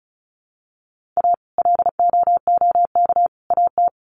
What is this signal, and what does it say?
Morse code, heard as a keyed tone, about 34 wpm, 700 Hz. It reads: A LOOK AT
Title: Sets of 3 Words 35wpm